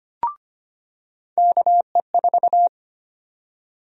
Morse code, heard as a keyed tone, about 25 wpm, 700 Hz. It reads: KE4
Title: USA Call Sign Prefixes - 25wpm